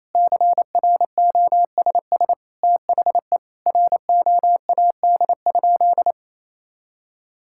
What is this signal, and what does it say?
Morse code, heard as a keyed tone, about 28 wpm, 700 Hz. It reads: CROSS THE ROAD?
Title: Sentences from Top 1000 Words - Review Part 5 - 28wpm